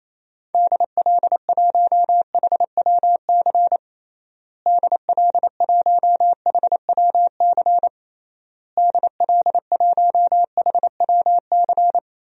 Morse code, heard as a keyed tone, about 28 wpm, 700 Hz. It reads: DL1HWC DL1HWC DL1HWC
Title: DXCC-100 Call Signs - 6 Characters - Repeated 3x - 28wpm